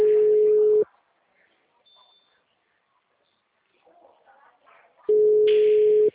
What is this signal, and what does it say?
Ringback tone (Europe, 425 Hz), with speech.
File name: out- -2006- - - .3901